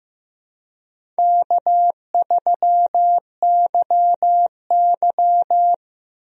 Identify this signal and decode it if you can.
Morse code, heard as a keyed tone, about 15 wpm, 700 Hz. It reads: K3YY
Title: USA Call Signs - 4 Characters - 15wpm